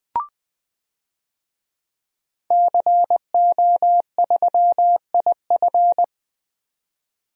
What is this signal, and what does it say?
Morse code, heard as a keyed tone, about 20 wpm, 700 Hz. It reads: CO3IF
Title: DXCC-100 Call Signs - 5 Characters - 20wpm